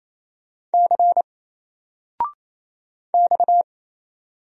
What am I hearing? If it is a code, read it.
Morse code, heard as a keyed tone, about 27 wpm, 700 Hz. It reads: C X
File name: Single-Letter-Number-28wpm